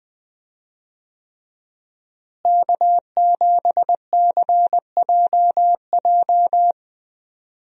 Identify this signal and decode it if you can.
Morse code, heard as a keyed tone, about 20 wpm, 700 Hz. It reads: K7CJJ
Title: USA Call Signs - 5 Characters - Repeated 3x - 20wpm